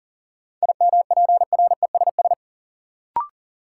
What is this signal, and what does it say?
Morse code, heard as a keyed tone, about 40 wpm, 700 Hz. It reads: IMPRESS
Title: Sentences from Top 1600 Words - New Word Drill - 40wpm